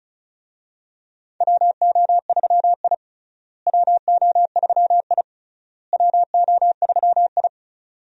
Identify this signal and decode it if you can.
Morse code, heard as a keyed tone, about 36 wpm, 700 Hz. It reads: WO3I WO3I WO3I
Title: USA Call Signs - 4 Characters - Repeated 3x - 35wpm